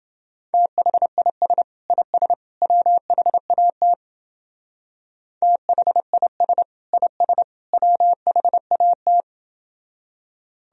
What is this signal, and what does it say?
Morse code, heard as a keyed tone, about 29 wpm, 700 Hz. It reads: THIS IS WHAT THIS IS WHAT